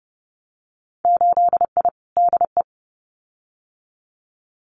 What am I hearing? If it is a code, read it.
Morse code, heard as a keyed tone, about 30 wpm, 700 Hz. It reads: 8I DE
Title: Winter Field Day Exchanges Encore 30wpm